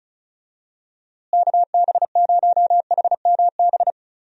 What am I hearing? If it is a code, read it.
Morse code, heard as a keyed tone, about 36 wpm, 700 Hz. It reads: KB0HMB
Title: USA Call Signs - 6 Characters - Repeated 2x - 35wpm